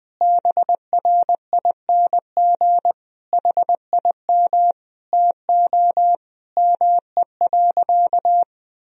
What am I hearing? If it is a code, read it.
Morse code, heard as a keyed tone, about 20 wpm, 700 Hz. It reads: BRING HIM TO ME.
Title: Sentences from Top 1000 Words - Review Part 3 - 20wpm